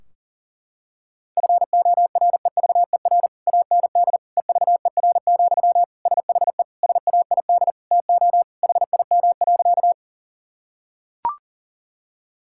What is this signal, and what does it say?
Morse code, heard as a keyed tone, about 37 wpm, 700 Hz. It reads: FOREVER AND EVER, SHE SAID TO HIM.
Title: Sentences from Top 1600 Words 40wpm